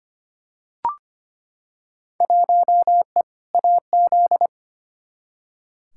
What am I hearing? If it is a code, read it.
Morse code, heard as a keyed tone, about 25 wpm, 700 Hz. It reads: 1E AZ
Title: Field Day Exchanges - 25wpm